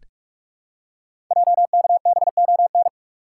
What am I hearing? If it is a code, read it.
Morse code, heard as a keyed tone, about 44 wpm, 700 Hz. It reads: JKBON
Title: Code Groups - 5 Letter:Numbers - 45wpm